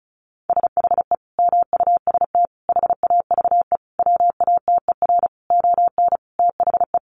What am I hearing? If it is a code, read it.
Morse code, heard as a keyed tone, about 34 wpm, 700 Hz. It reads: SHE MUST HAVE WATER ON THE
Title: Sentences from Top 1100 Words 35wpm